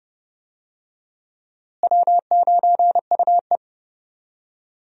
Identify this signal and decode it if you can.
Morse code, heard as a keyed tone, about 31 wpm, 700 Hz. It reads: W9UE
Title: USA Call Signs - 4 Characters - Repeated 3x - 30wpm